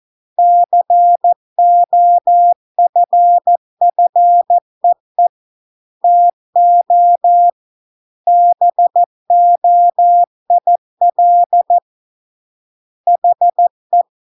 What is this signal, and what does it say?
Morse code, heard as a keyed tone, about 14 wpm, 700 Hz. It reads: COFFEE TO BOIL HE